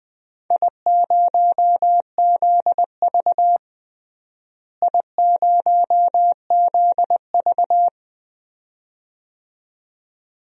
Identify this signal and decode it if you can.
Morse code, heard as a keyed tone, about 20 wpm, 700 Hz. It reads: I0ZV I0ZV
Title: DXCC-100 Call Signs - 4 Characters - Repeated 2x - 20wpm